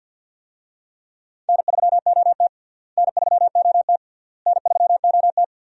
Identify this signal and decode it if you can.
Morse code, heard as a keyed tone, about 50 wpm, 700 Hz. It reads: N3OT N3OT N3OT
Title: USA Call Signs - 4 Characters - Repeated 3x - 50wpm